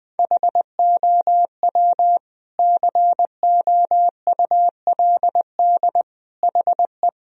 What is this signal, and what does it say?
Morse code, heard as a keyed tone, about 20 wpm, 700 Hz. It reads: HOW COULD HE
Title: Sentences from Top 1000 Words - Review Part 2 - 20wpm